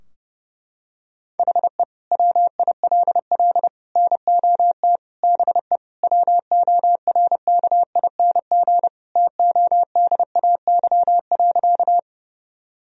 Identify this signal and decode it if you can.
Morse code, heard as a keyed tone, about 32 wpm, 700 Hz. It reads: HE WILL NOT BE WORKING TODAY.
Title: Sentences from Top 1000 Words - Review Part 4 - 30wpm